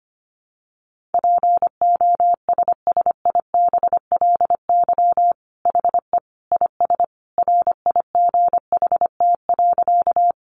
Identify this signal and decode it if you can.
Morse code, heard as a keyed tone, about 25 wpm, 700 Hz. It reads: POSSIBLY HE IS RIGHT.
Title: Sentences from Top 1300 Words 25wpm